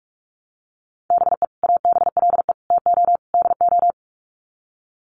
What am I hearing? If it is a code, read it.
Morse code, heard as a keyed tone, about 44 wpm, 700 Hz. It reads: BE ABLE TO DO
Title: Sets of 4 Words 45wpm